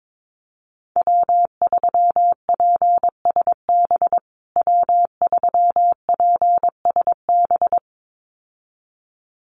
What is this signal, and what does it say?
Morse code, heard as a keyed tone, about 22 wpm, 700 Hz. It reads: W3PSB W3PSB
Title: USA Call Signs Repeated 2x 22wpm